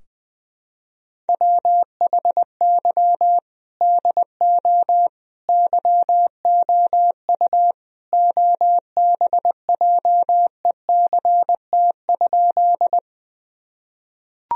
Morse code, heard as a keyed tone, about 20 wpm, 700 Hz. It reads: WHY DO YOU OBJECT?
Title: Sentences from Top 1000 Words - Review Part 3 - 20wpm